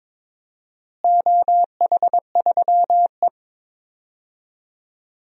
Morse code, heard as a keyed tone, about 22 wpm, 700 Hz. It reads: OH3E
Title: DXCC-100 Call Signs - 4 Characters - Repeated 3x - 22wpm